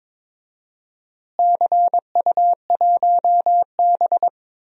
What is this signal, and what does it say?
Morse code, heard as a keyed tone, about 22 wpm, 700 Hz. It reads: CU1B
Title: DXCC-100 Call Signs - 4 Characters - Repeated 3x - 22wpm